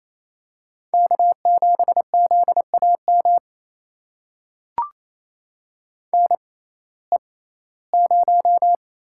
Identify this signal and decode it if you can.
Morse code, heard as a keyed tone, about 28 wpm, spaced as for 17 wpm, 700 Hz. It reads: K7ZAM NE0
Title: USA Call Signs - 5 Characters - 6x Character Spacing - 28wpm